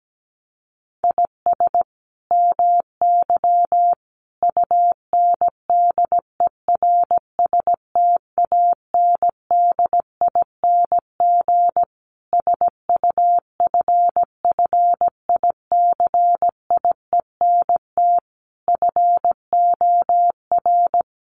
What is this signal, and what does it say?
Morse code, heard as a keyed tone, about 17 wpm, 700 Hz. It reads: IS MY UNDERSTANDING SUFFICIENT FOR
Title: Sentences from Top 1400 Words 17wpm